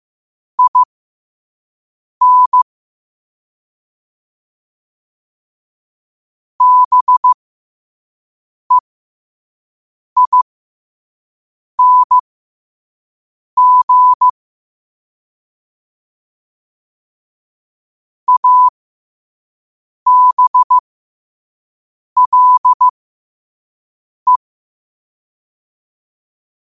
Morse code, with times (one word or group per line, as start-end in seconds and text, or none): 0.6-2.6 IN
6.6-14.3 BEING
18.3-24.4 ABLE